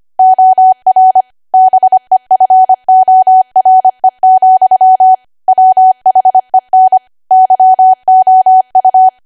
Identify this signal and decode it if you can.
Morse code, heard as a keyed tone, about 25 wpm, 750 Hz. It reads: OR BEFORE, WHEN YOU
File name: Archive / 25 WPM / 25WPM